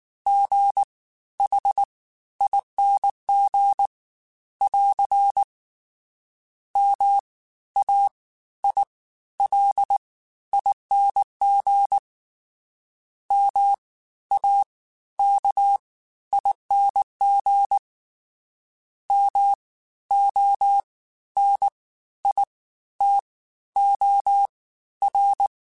Morse code, heard as a keyed tone, about 19 wpm, spaced as for 12 wpm, 800 Hz. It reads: GHING <AR> MAILING MAKING MONITOR